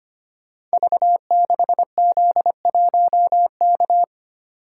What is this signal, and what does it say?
Morse code, heard as a keyed tone, about 25 wpm, 700 Hz. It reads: V6Z1K
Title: Code Groups - 5 Letter:Numbers - 25wpm